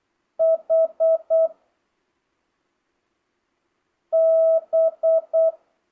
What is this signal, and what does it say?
Morse code, heard as a keyed tone, about 8 wpm, 650 Hz. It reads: H B
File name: jh8cbh / 2025cw